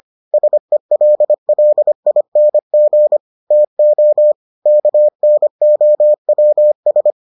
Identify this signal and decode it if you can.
Morse code, heard as a keyed tone, about 25 wpm, 600 Hz. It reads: SELLING TO KNOWS